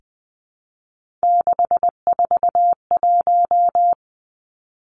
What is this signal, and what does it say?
Morse code, heard as a keyed tone, about 19 wpm, 700 Hz. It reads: 641
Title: Three Challenging Letter-Number Combinations 20wpm